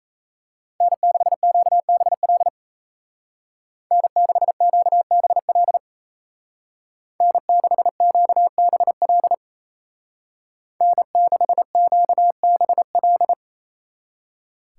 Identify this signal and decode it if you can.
Morse code, heard as a keyed tone, about 33 wpm, 700 Hz. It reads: N6QBL N6QBL N6QBL N6QBL
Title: Speed Racing - USA 1x3 Call Signs 28wpm